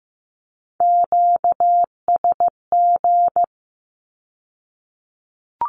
Morse code, heard as a keyed tone, about 15 wpm, 700 Hz. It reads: QSG
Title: Q-Codes with Spoken Meaning - 15wpm